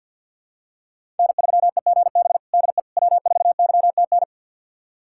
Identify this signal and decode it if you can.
Morse code, heard as a keyed tone, about 46 wpm, 690 Hz. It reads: N3EGB DE W4YTN
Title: USA Callsign DE Callsign - 5 Characters - 50wpm